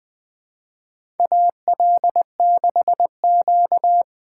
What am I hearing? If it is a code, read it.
Morse code, heard as a keyed tone, about 20 wpm, 700 Hz. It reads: AL6Q
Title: USA Call Signs - 4 Characters - 3x Character Spacing - 20wpm